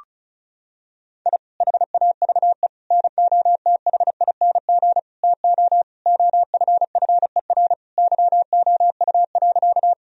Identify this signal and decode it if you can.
Morse code, heard as a keyed tone, about 36 wpm, 700 Hz. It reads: I HAVE NOTHING TO OFFER YOU.
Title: Sentences from Top 1000 Words - Review Part 4 - 35wpm